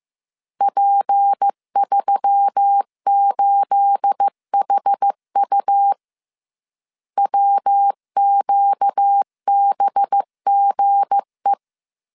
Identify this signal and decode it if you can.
Morse code, heard as a keyed tone, about 15 wpm, 800 Hz. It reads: P38HU WQBGE
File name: CW / DET / 1 9 1500C S C 14wpm 3245